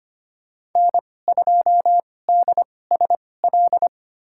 Morse code, heard as a keyed tone, about 25 wpm, spaced as for 18 wpm, 700 Hz. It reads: N2DSL
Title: USA Call Signs - 2x Character Spacing - 25wpm